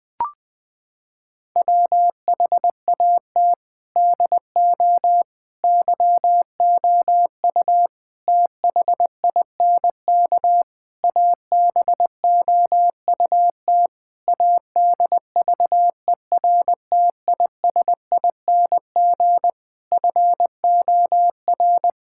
Morse code, heard as a keyed tone, about 21 wpm, 700 Hz. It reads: WHAT DO YOU THINK ABOUT ADVERTISING FOR
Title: Sentences from Top 1000 Words - Review Part 6 - 20wpm